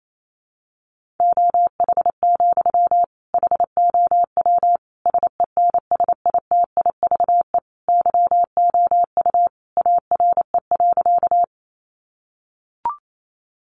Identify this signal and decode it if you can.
Morse code, heard as a keyed tone, about 28 wpm, 700 Hz. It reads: OH, HOW SENSITIVE YOU ARE.
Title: Sentences from Top 1400 Words 28wpm